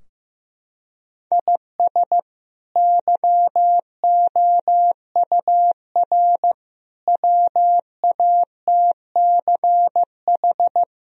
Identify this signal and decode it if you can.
Morse code, heard as a keyed tone, about 15 wpm, 700 Hz. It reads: IS YOUR WATCH
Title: Sentences from Top 1000 Words - Review Part 2 - 15wpm